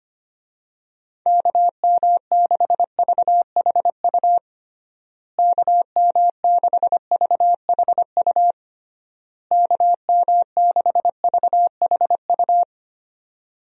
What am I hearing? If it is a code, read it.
Morse code, heard as a keyed tone, about 25 wpm, 700 Hz. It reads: KM6VHU KM6VHU KM6VHU
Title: USA Call Signs - 6 Characters - Repeated 3x - 25wpm